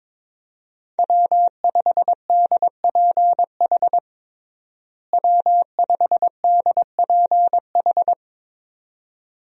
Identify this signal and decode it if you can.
Morse code, heard as a keyed tone, about 21 wpm, 700 Hz. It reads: W5DPH W5DPH
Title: USA Call Signs - 5 Characters - Repeated 2x - 22wpm